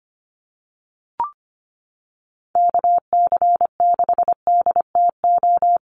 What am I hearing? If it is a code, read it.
Morse code, heard as a keyed tone, about 25 wpm, 700 Hz. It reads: KC6DTO
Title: USA Call Signs Repeated 3x 25wpm